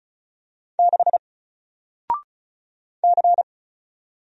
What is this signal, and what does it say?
Morse code, heard as a keyed tone, about 36 wpm, 700 Hz. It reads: 6 C
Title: Single Letter-Number 35wpm